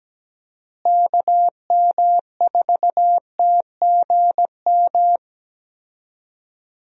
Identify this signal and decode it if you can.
Morse code, heard as a keyed tone, about 17 wpm, 700 Hz. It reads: KM4TGM
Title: USA Call Signs - 6 Characters - Repeated 2x - 17wpm